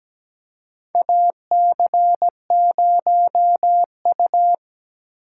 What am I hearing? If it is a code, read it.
Morse code, heard as a keyed tone, about 17 wpm, 700 Hz. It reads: AC0U
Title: USA Call Signs - 4 Characters - 2x Character Spacing - 17wpm